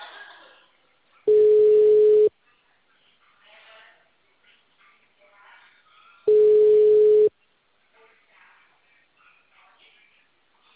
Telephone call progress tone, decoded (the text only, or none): ringback tone